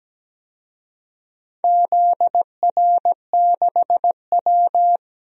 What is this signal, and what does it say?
Morse code, heard as a keyed tone, about 17 wpm, 700 Hz. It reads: ZR6W
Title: DXCC-100 Call Signs - 4 Characters - 17wpm